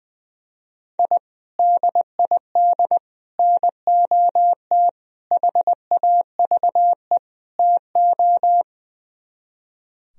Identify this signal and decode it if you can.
Morse code, heard as a keyed tone, about 20 wpm, 700 Hz. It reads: I DID NOT HAVE TO
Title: Speed-Racing - Sets of 5 Words 20wpm